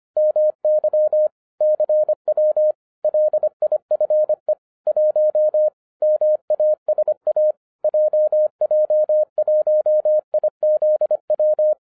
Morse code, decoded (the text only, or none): MY CW LIFE 1 MASA JJ1IZW